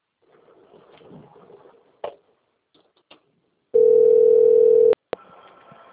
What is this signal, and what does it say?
Ringback tone (North America), 440 + 480 Hz.
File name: out- - - - - .10441